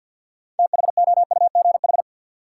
Morse code, heard as a keyed tone, about 53 wpm, 700 Hz. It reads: THOUGH
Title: Sentences from Top 400 Words - New Word Drill - 50wpm